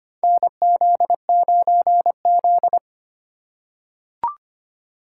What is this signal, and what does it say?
Morse code, heard as a keyed tone, about 25 wpm, 700 Hz. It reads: NZ9Z
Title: USA Call Signs - 4 Characters - 3x Character Spacing - 25wpm